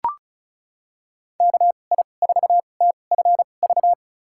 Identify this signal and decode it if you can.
Morse code, heard as a keyed tone, about 34 wpm, spaced as for 25 wpm, 700 Hz. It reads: KI4TFV
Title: USA Call Signs - 6 Characters - 2x Character Spacing - 35wpm